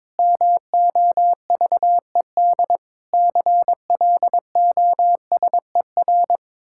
Morse code, heard as a keyed tone, about 22 wpm, 700 Hz. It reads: MOVED CLOSER
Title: Pangram Challenge v2 - 22wpm